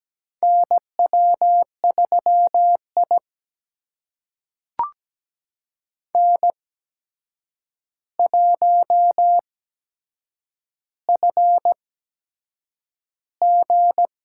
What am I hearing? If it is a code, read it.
Morse code, heard as a keyed tone, about 17 wpm, spaced as for 7 wpm, 700 Hz. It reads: NW3I N1FG
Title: USA Call Signs - 4 Characters - 8x Character Spacing - 17wpm